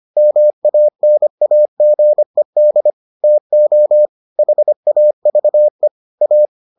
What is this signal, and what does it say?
Morse code, heard as a keyed tone, about 25 wpm, 600 Hz. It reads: MANAGED TO HAVE A